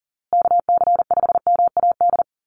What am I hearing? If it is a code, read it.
Morse code, heard as a keyed tone, about 40 wpm, 700 Hz. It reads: KC5MAD
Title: USA Call Signs Repeated 3x 40wpm